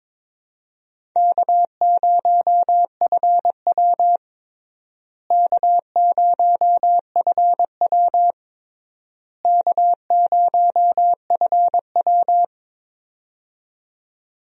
Morse code, heard as a keyed tone, about 22 wpm, 700 Hz. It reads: K0FW K0FW K0FW
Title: USA Call Signs - 4 Characters - Repeated 3x - 22wpm